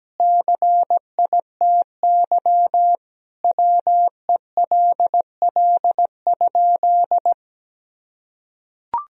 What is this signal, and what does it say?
Morse code, heard as a keyed tone, about 17 wpm, 700 Hz. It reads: CITY WELL?
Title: Sentences from Top 1000 Words - Review Part 2 - 17wpm